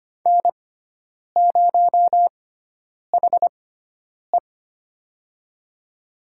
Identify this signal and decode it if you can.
Morse code, heard as a keyed tone, about 25 wpm, spaced as for 9 wpm, 700 Hz. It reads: N0HE
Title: USA Call Signs - 4 Characters - 6x Character Spacing - 25wpm